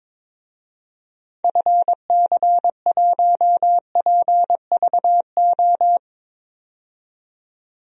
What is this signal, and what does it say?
Morse code, heard as a keyed tone, about 21 wpm, 700 Hz. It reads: FC1PVO
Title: DXCC-100 Call Signs - 6 Characters - 22wpm